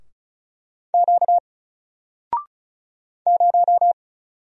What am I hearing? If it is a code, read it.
Morse code, heard as a keyed tone, about 34 wpm, 700 Hz. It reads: Q 0
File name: Single-Letter-Number-35wpm